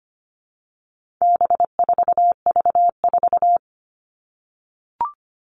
Morse code, heard as a keyed tone, about 25 wpm, 700 Hz. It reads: B4V4
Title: All Cut Numbers - 4-Digit 25wpm